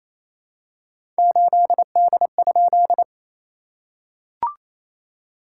Morse code, heard as a keyed tone, about 28 wpm, 690 Hz. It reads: 8D?
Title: SOTA QSO Practice 28wpm